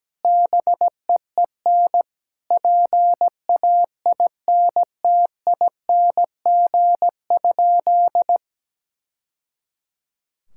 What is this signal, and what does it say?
Morse code, heard as a keyed tone, about 17 wpm, 700 Hz. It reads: BEEN PAINTING?
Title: Sentences from Top 1000 Words - Review Part 6 - 17wpm